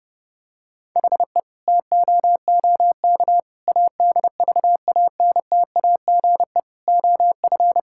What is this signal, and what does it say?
Morse code, heard as a keyed tone, about 30 wpm, 700 Hz. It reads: HE TOOK ADVANTAGE OF
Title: Sentences from Top 1000 Words - Review Part 6 - 30wpm